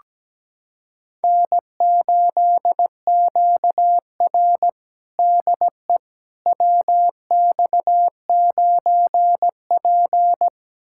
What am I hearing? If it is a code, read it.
Morse code, heard as a keyed tone, about 17 wpm, 700 Hz. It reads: N8QR DE WX9P
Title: USA Callsign DE Callsign - 4 Characters - 17wpm